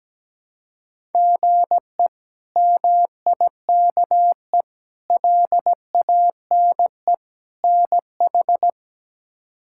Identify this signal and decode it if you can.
Morse code, heard as a keyed tone, about 17 wpm, 700 Hz. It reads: GE MIKE LANE NH
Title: K1USN SST QSO Practice - 17wpm